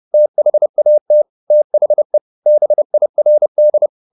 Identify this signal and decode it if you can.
Morse code, heard as a keyed tone, about 30 wpm, 600 Hz. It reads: THAT THE BIRD
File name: the old man and the sea / the old man and the sea 1 30wpm